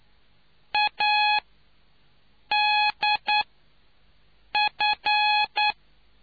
Morse code, heard as a keyed tone, about 9 wpm, spaced as for 5 wpm, 840 Hz. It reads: ADF